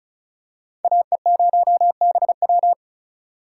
Morse code, heard as a keyed tone, about 36 wpm, 700 Hz. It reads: AE0BW
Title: USA Call Signs - 4x Character Spacing - 35wpm